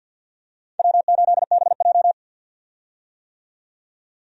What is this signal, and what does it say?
Morse code, heard as a keyed tone, about 48 wpm, 700 Hz. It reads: W8BJ
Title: USA Call Signs - 4 Characters - 50wpm